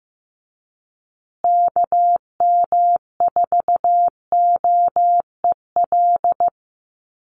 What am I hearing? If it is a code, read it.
Morse code, heard as a keyed tone, about 15 wpm, 700 Hz. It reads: KM4OEL
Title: USA Call Signs Repeated 2x 15wpm